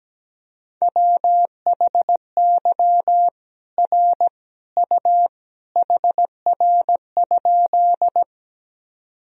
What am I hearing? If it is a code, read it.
Morse code, heard as a keyed tone, about 17 wpm, 700 Hz. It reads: WHY R U HR?
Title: Sentences from Top 200 Words and Abbreviations 17wpm